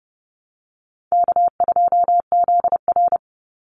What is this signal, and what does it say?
Morse code, heard as a keyed tone, about 30 wpm, 700 Hz. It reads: K2ZR
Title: Call Signs - 30wpm